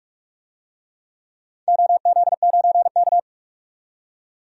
Morse code, heard as a keyed tone, about 47 wpm, 700 Hz. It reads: OZ9K
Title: DXCC-100 Call Signs - 4 Characters - 45wpm